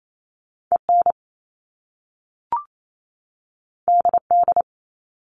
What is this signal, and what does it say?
Morse code, heard as a keyed tone, about 28 wpm, 700 Hz. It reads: EN DD
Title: Top Two Letter Combinations in Words 28wpm